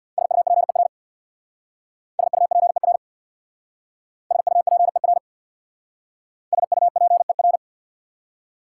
Morse code, heard as a keyed tone, about 63 wpm, 690 Hz. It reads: SUPER SUPER SUPER SUPER